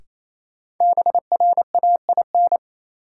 Morse code, heard as a keyed tone, about 28 wpm, 700 Hz. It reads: BRAIN
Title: Sentences from Top 1100 Words - New Word Drill Alt - 28wpm